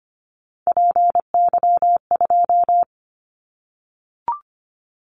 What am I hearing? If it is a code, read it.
Morse code, heard as a keyed tone, about 25 wpm, 700 Hz. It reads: PY2
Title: Callsign Prefixes for Top 50 DXCC - 25wpm